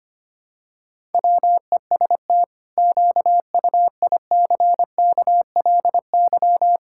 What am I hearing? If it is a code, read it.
Morse code, heard as a keyed tone, about 25 wpm, 700 Hz. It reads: WEST QUICKLY
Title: Pangram Challenge v2 - 25wpm